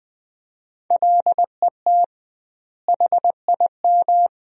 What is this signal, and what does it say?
Morse code, heard as a keyed tone, about 20 wpm, 700 Hz. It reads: LET HIM
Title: 2x Word-Spacing - Sentences from Top 300 Words - 20wpm